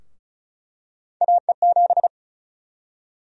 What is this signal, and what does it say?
Morse code, heard as a keyed tone, about 36 wpm, 700 Hz. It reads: AE7
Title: USA Call Sign Prefixes - Repeated 2x - 35wpm